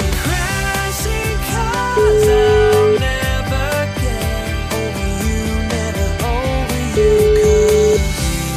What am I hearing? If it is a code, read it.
Ringback tone (Europe, 425 Hz), with noise.